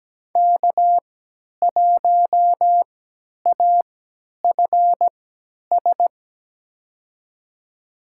Morse code, heard as a keyed tone, about 17 wpm, spaced as for 10 wpm, 700 Hz. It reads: K1AFS
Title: USA Call Signs - 5 Characters - 3x Character Spacing - 17wpm